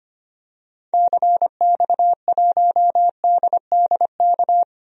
Morse code, heard as a keyed tone, about 25 wpm, 700 Hz. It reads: CX1DDK
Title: DXCC-100 Call Signs - 6 Characters - Repeated 3x - 25wpm